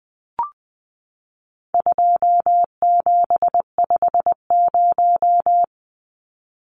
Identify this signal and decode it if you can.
Morse code, heard as a keyed tone, about 20 wpm, 700 Hz. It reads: 2750
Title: Numbers - 4-Digit 20wpm